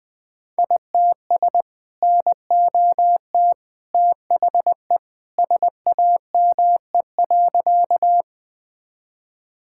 Morse code, heard as a keyed tone, about 21 wpm, 700 Hz. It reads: ITS NOT THE SAME.
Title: Sentences with 2-Word Phrases - 20wpm